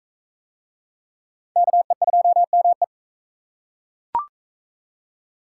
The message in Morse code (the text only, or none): KE2ME